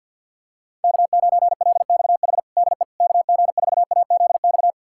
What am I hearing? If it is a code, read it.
Morse code, heard as a keyed tone, about 52 wpm, 700 Hz. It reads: K9LXH DE KG4AZX
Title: USA Callsign DE Callsign 50wpm